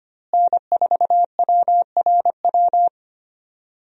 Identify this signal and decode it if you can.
Morse code, heard as a keyed tone, about 25 wpm, 700 Hz. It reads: N4WRW
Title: USA Call Signs - 5 Characters - Repeated 3x - 25wpm